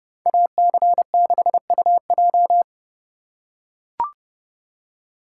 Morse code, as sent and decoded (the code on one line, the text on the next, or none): .- -.-. -.... ..- .---
AC6UJ